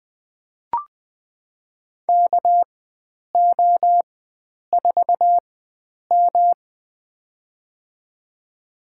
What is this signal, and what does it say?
Morse code, heard as a keyed tone, about 20 wpm, spaced as for 9 wpm, 700 Hz. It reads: KO4M
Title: USA Call Signs - 4x Character Spacing - 20wpm